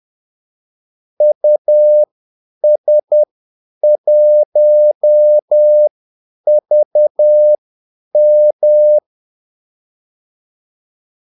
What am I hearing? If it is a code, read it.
Morse code, heard as a keyed tone, about 10 wpm, spaced as for 8 wpm, 600 Hz. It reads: US1VM
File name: Callsigns-10-8wpm